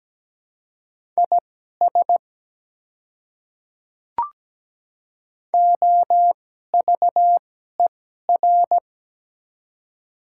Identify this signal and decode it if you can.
Morse code, heard as a keyed tone, about 17 wpm, spaced as for 12 wpm, 700 Hz. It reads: IS OVER
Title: 2x Character Spacing - Top 100 Words - 17wpm